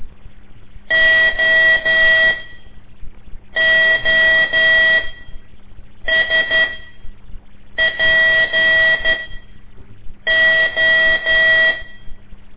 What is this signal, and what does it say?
Morse code, heard as a keyed tone, about 12 wpm, spaced as for 6 wpm, 1950 Hz. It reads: OOSPO
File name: ik1weg2 / CORSI CW / lez5-1